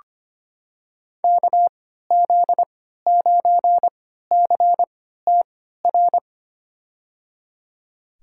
Morse code, heard as a keyed tone, about 25 wpm, spaced as for 14 wpm, 700 Hz. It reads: KZ9CTR
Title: USA Call Signs - 3x Character Spacing - 25wpm